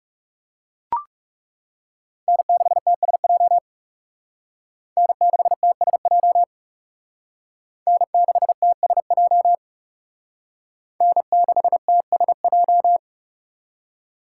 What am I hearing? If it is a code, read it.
Morse code, heard as a keyed tone, about 39 wpm, 700 Hz. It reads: N6TSJ N6TSJ N6TSJ N6TSJ